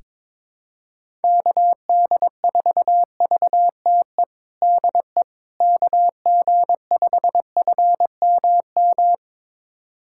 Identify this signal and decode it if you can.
Morse code, heard as a keyed tone, about 21 wpm, 700 Hz. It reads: KD4VTE DE KG5FMM
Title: USA Callsign DE Callsign - 6 Characters - 22wpm